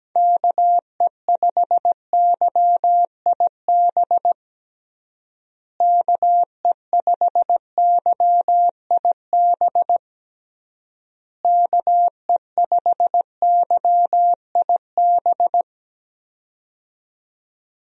Morse code, heard as a keyed tone, about 17 wpm, 700 Hz. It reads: KE5YIB KE5YIB KE5YIB
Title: USA Call Signs - 6 Characters - Repeated 3x - 17wpm